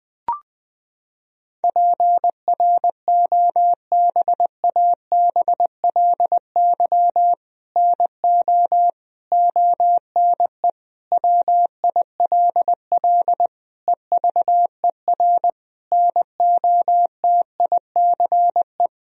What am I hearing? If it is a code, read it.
Morse code, heard as a keyed tone, about 21 wpm, 700 Hz. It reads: PROBABLY NO ONE WILL EVER NOTICE
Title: Sentences from Top 1000 Words - Review Part 4 - 20wpm